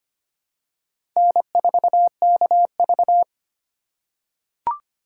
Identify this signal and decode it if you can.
Morse code, heard as a keyed tone, about 25 wpm, 700 Hz. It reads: N4KV
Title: USA Call Signs - 4 Characters - 3x Character Spacing - 25wpm